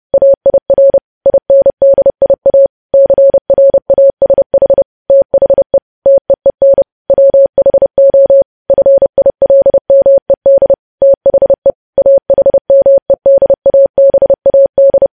Morse code, heard as a keyed tone, about 30 wpm, 550 Hz. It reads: AIR INDIA CRASH THE TEEN WHO FILMED THE AHMEDABAD
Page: news headlines in morse code at 30 wpm